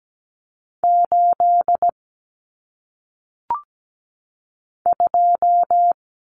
Morse code, heard as a keyed tone, about 17 wpm, 700 Hz. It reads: 8 2
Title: Common Cut Numbers - 1-Digit 17wpm